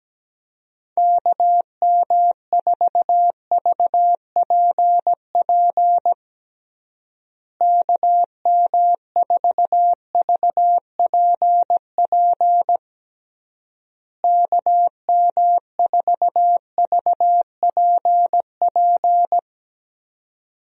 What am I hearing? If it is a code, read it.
Morse code, heard as a keyed tone, about 17 wpm, 700 Hz. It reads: KM4VPP KM4VPP KM4VPP
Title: USA Call Signs - 6 Characters - Repeated 3x - 17wpm